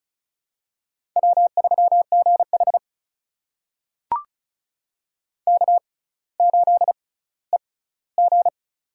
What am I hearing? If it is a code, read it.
Morse code, heard as a keyed tone, about 36 wpm, spaced as for 18 wpm, 700 Hz. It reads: W3GH K8EG
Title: USA Call Signs - 4 Characters - 6x Character Spacing - 35wpm